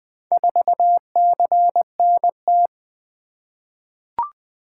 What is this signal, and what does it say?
Morse code, heard as a keyed tone, about 20 wpm, 700 Hz. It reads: 4CNT
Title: Four Letter-Number Combinations - 20wpm